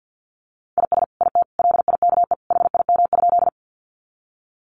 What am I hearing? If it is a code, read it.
Morse code, heard as a keyed tone, about 50 wpm, 700 Hz. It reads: IS IT LIKE HIM?
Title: Sentences from Top 100 Words 50wpm